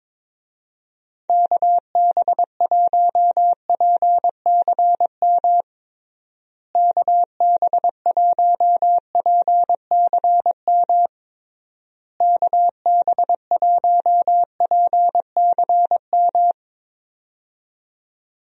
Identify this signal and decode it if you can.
Morse code, heard as a keyed tone, about 22 wpm, 700 Hz. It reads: KB1PCM KB1PCM KB1PCM
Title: USA Call Signs - 6 Characters - Repeated 3x - 22wpm